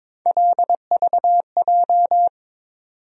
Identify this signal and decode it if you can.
Morse code, heard as a keyed tone, about 22 wpm, 700 Hz. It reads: LVJ
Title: USA Call Sign Postfixes - 3 Letters 22wpm